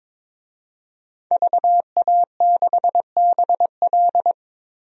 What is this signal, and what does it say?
Morse code, heard as a keyed tone, about 22 wpm, 700 Hz. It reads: VA6BL
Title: DXCC-100 Call Signs - 5 Characters - 22wpm